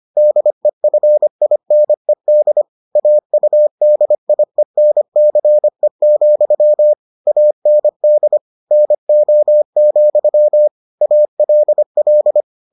Morse code, heard as a keyed tone, about 25 wpm, 600 Hz. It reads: DEFINED AUDIENCE, AND NO, ALL